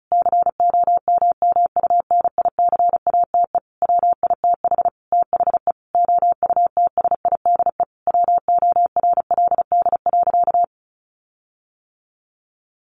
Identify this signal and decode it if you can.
Morse code, heard as a keyed tone, about 34 wpm, 700 Hz. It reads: COMMUNICATE WITH THE OUTSIDE WORLD.
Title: Sentences from Top 1300 Words 35wpm